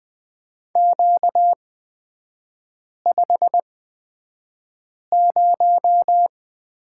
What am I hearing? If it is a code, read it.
Morse code, heard as a keyed tone, about 20 wpm, spaced as for 5 wpm, 700 Hz. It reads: Q50
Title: 236 Lesson 38e - Letter q - Three Characters 20wpm